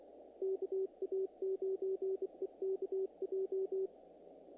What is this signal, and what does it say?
Morse code, heard as a keyed tone, about 24 wpm, 380 Hz. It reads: KA9EKJ
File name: contests / naqp-cw-k5tr-op-k5pi- -1800